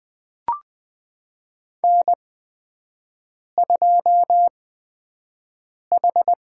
Morse code, heard as a keyed tone, about 20 wpm, spaced as for 5 wpm, 700 Hz. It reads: N2H
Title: USA Call Signs - 8x Character Spacing - 20wpm